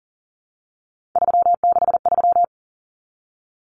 Morse code, heard as a keyed tone, about 40 wpm, 700 Hz. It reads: 363
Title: Common Cut Numbers - 3-Digit 40wpm